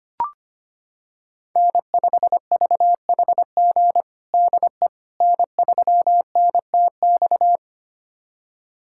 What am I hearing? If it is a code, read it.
Morse code, heard as a keyed tone, about 25 wpm, 700 Hz. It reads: N5VHG DE N3NTX
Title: USA Callsign DE Callsign - 5 Characters - 25wpm